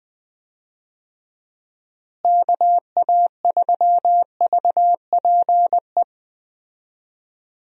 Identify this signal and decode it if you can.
Morse code, heard as a keyed tone, about 20 wpm, 700 Hz. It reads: KA3VPE
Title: USA Call Signs - 6 Characters - Repeated 3x - 20wpm